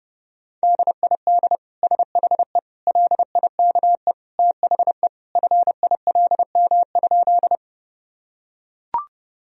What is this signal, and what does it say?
Morse code, heard as a keyed tone, about 30 wpm, 700 Hz. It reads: DID SHE LIKE THE FILM?
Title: Sentences from Top 1000 Words - Review Part 3 - 30wpm